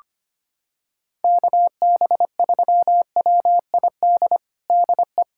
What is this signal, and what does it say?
Morse code, heard as a keyed tone, about 25 wpm, 700 Hz. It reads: KB3WID DE
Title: USA Callsign DE Callsign 25wpm